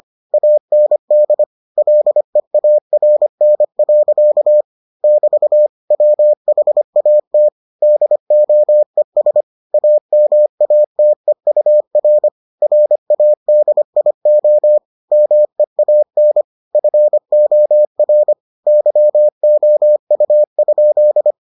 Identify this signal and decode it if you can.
Morse code, heard as a keyed tone, about 25 wpm, 600 Hz. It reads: AND LEARN. <BT> WHAT DOES AMATEUR RADIO MEAN FOR YOU?